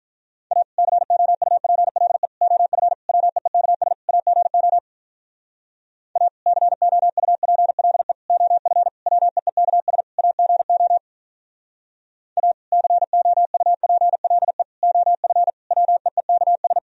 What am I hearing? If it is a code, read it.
Morse code, heard as a keyed tone, about 43 wpm, 700 Hz. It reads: A COUPLE OF WEEKS AGO A COUPLE OF WEEKS AGO A COUPLE OF WEEKS
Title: Speed-Racing - Sets of 5 Words 35wpm